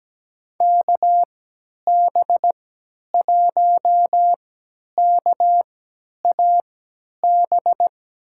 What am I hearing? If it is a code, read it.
Morse code, heard as a keyed tone, about 17 wpm, spaced as for 10 wpm, 700 Hz. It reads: KB1KAB
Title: USA Call Signs - 6 Characters - 3x Character Spacing - 17wpm